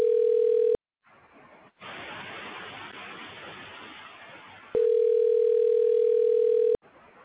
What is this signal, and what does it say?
Ringback tone (North America), 440 + 480 Hz.
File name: recordings / REad0e cda1ac92c7d093eee85c0